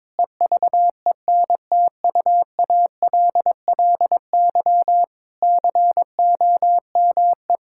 Morse code, heard as a keyed tone, about 22 wpm, 700 Hz. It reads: EVENTUALLY COME
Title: Sentences from Top 1000 Words - Review Part 6 - 22wpm